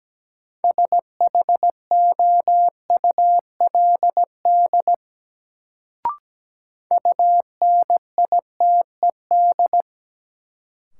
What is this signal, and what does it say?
Morse code, heard as a keyed tone, about 17 wpm, 700 Hz. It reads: SHOULD UNITED
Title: Sets of 2 Words - New Word Drill - 17wpm